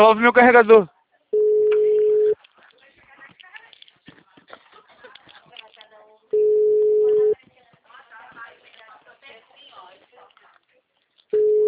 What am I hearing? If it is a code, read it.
Ringback tone (Europe, 425 Hz), with speech.